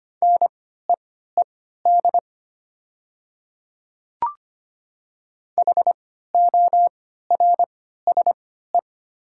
Morse code, heard as a keyed tone, about 25 wpm, spaced as for 14 wpm, 700 Hz. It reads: NEED HORSE
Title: 3x Character Spacing - Top 500 Words - 25wpm